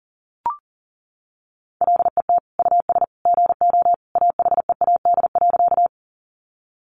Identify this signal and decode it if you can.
Morse code, heard as a keyed tone, about 40 wpm, 700 Hz. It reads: LET US GO AHEAD.
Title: Sentences from Top 1100 Words 40wpm